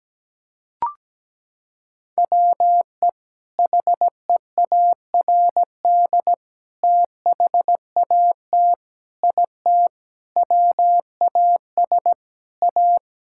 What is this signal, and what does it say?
Morse code, heard as a keyed tone, about 17 wpm, 700 Hz. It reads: WE HEARD THAT IT WAS A
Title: Sentences from Top 1000 Words - Review Part 6 - 17wpm